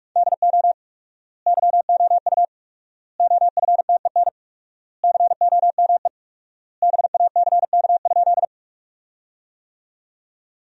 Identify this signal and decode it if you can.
Morse code, heard as a keyed tone, about 42 wpm, 700 Hz. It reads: DO YOU OFTEN COME BACK?
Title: 4x Word-Spacing - Sentences from Top 300 Words - 45wpm